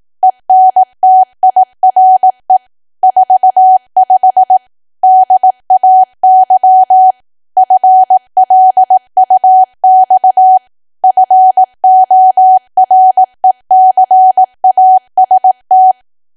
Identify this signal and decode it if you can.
Morse code, heard as a keyed tone, about 18 wpm, 750 Hz. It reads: ENTIRE 45 DAY FLUX FORECAST
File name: cwa / W1AW-Level2 / 18WPM